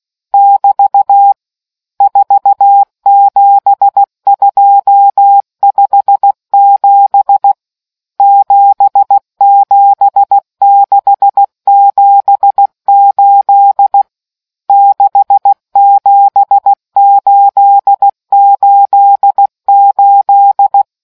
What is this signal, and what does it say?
Morse code, heard as a keyed tone, about 16 wpm, 800 Hz. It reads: <BT> 47257 77678 67888